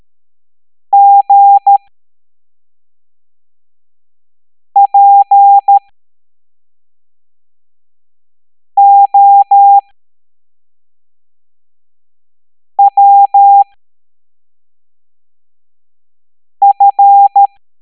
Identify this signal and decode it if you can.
Morse code, heard as a keyed tone, about 13 wpm, spaced as for 3 wpm, 800 Hz. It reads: GPOWF
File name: hamdata / L07-EX03-HQP